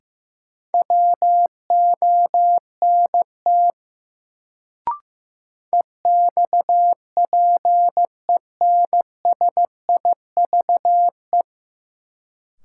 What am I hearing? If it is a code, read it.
Morse code, heard as a keyed tone, about 16 wpm, 700 Hz. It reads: WONT EXPENSIVE
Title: Sentences from Top 1000 Words - Review Part 6 - New Word Drill - 15wpm